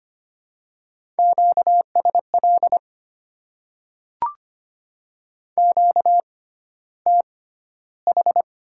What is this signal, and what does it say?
Morse code, heard as a keyed tone, about 25 wpm, spaced as for 13 wpm, 700 Hz. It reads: QSL QTH
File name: Top 90 QSO Elements - 6x Character Spacing - 25wpm